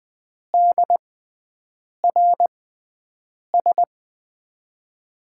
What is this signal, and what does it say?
Morse code, heard as a keyed tone, about 20 wpm, spaced as for 7 wpm, 700 Hz. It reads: DRS
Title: USA Call Signs - 5 Characters - 6x Character Spacing - 20wpm